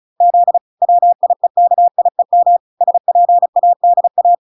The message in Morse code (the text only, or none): Z WIEKIEM SPADA